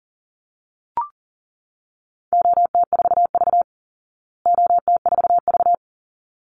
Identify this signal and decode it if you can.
Morse code, heard as a keyed tone, about 40 wpm, 700 Hz. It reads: OT4V OT4V
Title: Call Signs Repeated 3x - 40wpm